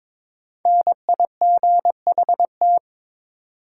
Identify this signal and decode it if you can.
Morse code, heard as a keyed tone, about 21 wpm, 700 Hz. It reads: NIGHT
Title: Sets of 4 Words - New Word Drill - 22wpm